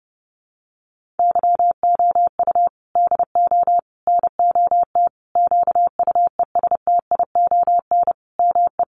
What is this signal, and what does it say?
Morse code, heard as a keyed tone, about 30 wpm, 700 Hz. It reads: YOU DO NOT QUESTION ME
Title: Sentences from Top 1100 Words 30wpm